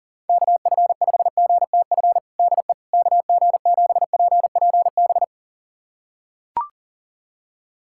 Morse code, heard as a keyed tone, about 40 wpm, 700 Hz. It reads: KF5GTF DE KG7PPB
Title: USA Callsign DE Callsign 40wpm